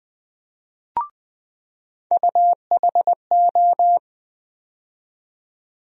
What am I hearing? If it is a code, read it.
Morse code, heard as a keyed tone, about 21 wpm, 700 Hz. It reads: UHO
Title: USA Call Sign Postfixes - 3 Letters 20wpm